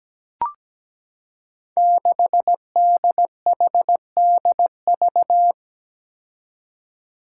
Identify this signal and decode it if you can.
Morse code, heard as a keyed tone, about 17 wpm, 700 Hz. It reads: 6DHDV
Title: Five Challenging Letter-Number Combinations 17wpm